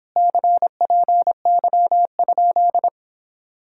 Morse code, heard as a keyed tone, about 31 wpm, 700 Hz. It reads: CPY?